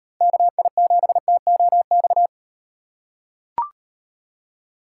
Morse code, heard as a keyed tone, about 38 wpm, 700 Hz. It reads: KI7TOX